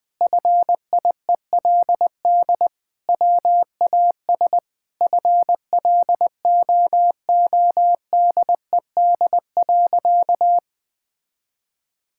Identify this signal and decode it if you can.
Morse code, heard as a keyed tone, about 20 wpm, 700 Hz. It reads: FIELD WAS FLOODED.
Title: Sentences with 6-Word Phrases - Spoken Beginning - 20wpm